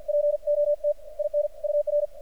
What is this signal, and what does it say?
Morse code, heard as a keyed tone, about 50 wpm, 590 Hz, with radio noise. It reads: POT IN UR
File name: archived / CRSnet